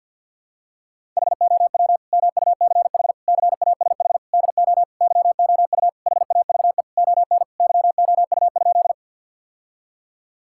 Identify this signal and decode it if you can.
Morse code, heard as a keyed tone, about 50 wpm, 700 Hz. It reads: HOW MUCH CASH DO YOU HAVE ON YOU?
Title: Sentences from Top 1000 Words - Review Part 5 - 50wpm